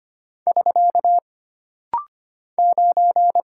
Morse code, heard as a keyed tone, about 25 wpm, 700 Hz. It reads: <SK> 9
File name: Single Character (Letter:Number:Punctuation:Prosign) - Rapid-Fire - 25wpm